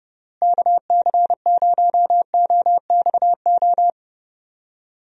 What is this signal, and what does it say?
Morse code, heard as a keyed tone, about 30 wpm, 700 Hz. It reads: KC0OXO
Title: USA Call Signs - 6 Characters - Repeated 2x - 30wpm